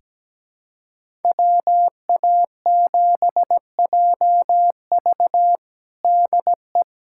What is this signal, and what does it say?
Morse code, heard as a keyed tone, about 17 wpm, 700 Hz. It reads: WA7JV DE
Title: USA Callsign DE Callsign - 5 Characters - 17wpm